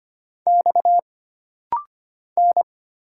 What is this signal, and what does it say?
Morse code, heard as a keyed tone, about 25 wpm, 700 Hz. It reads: X N
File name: Single Character (Letter:Number:Punctuation:Prosign) - Rapid-Fire - 25wpm